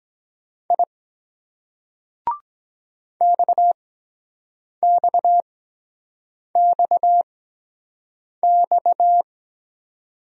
Morse code, heard as a keyed tone, about 23 wpm, spaced as for 7 wpm, 700 Hz. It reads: I XXXX